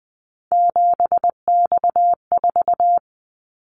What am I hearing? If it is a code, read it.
Morse code, heard as a keyed tone, about 20 wpm, 700 Hz. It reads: 7X4
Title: Callsign Prefixes for Top 100 DXCC - 20wpm